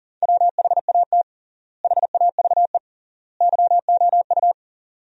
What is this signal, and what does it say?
Morse code, heard as a keyed tone, about 40 wpm, 700 Hz. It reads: WHAT HAVE YOU
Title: 3x Word-Spacing - Sentences from Top 400 Words - 40wpm